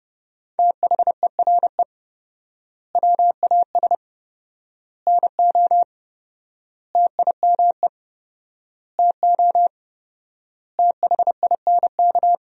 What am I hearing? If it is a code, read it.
Morse code, heard as a keyed tone, about 30 wpm, 700 Hz. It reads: THERE WAS NO TIME TO THINK
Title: 4x Word-Spacing - Sentences from Top 400 Words - 30wpm